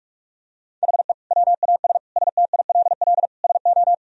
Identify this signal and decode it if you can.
Morse code, heard as a keyed tone, about 43 wpm, 700 Hz. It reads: HE WAS STILL SO
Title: Sentences from Top 1000 Words - Review Part 2 - 45wpm